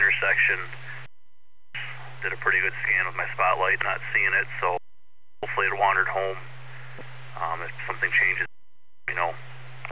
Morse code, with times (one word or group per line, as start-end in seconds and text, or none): none